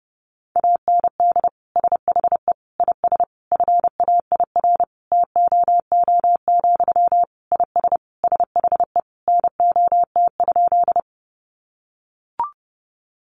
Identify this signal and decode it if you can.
Morse code, heard as a keyed tone, about 30 wpm, 700 Hz. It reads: AND SHE IS FAIR TOO, IS SHE NOT?
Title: Sentences from Top 1300 Words 30wpm